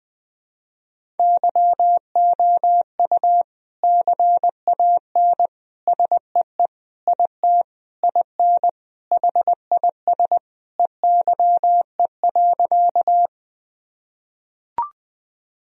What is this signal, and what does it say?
Morse code, heard as a keyed tone, about 20 wpm, 700 Hz. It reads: YOU CAN SEE IT IN HIS EYE.
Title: Sentences from Top 1000 Words - Review Part 1 - 20wpm